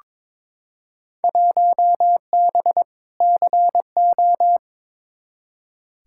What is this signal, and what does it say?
Morse code, heard as a keyed tone, about 21 wpm, 700 Hz. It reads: 1B CO
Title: Field Day Exchanges - 22wpm